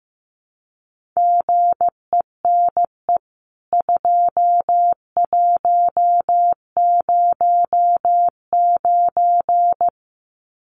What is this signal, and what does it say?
Morse code, heard as a keyed tone, about 14 wpm, 700 Hz. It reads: GENE 2109
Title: CWT Exchange 15wpm